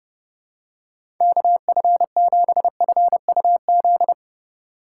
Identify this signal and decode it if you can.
Morse code, heard as a keyed tone, about 32 wpm, 700 Hz. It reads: KF7FUZ
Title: USA Call Signs - 6 Characters - 3x Character Spacing - 30wpm